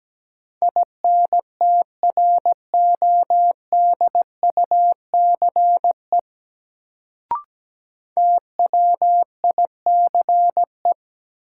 Morse code, heard as a keyed tone, about 17 wpm, 700 Hz. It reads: INTRODUCE TWICE
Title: Sentences from Top 1600 Words - New Word Drill Alt - 17wpm